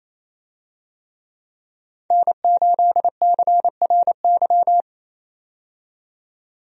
Morse code, heard as a keyed tone, about 28 wpm, 700 Hz. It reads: N8CRY
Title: USA Call Signs - 5 Characters - 28wpm